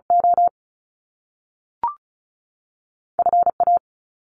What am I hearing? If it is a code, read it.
Morse code, heard as a keyed tone, about 36 wpm, spaced as for 6 wpm, 700 Hz. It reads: OFA
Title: Top Two Letter Combinations in Words 35wpm